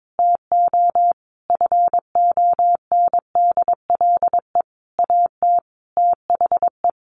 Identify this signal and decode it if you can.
Morse code, heard as a keyed tone, about 22 wpm, 700 Hz. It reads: TO FONDLE AT THE
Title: A Sound Of Thunder 22wpm